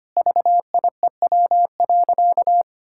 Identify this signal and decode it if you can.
Morse code, heard as a keyed tone, about 25 wpm, 700 Hz. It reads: VIEW.
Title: Sentences from Top 1000 Words - Review Part 6 - 25wpm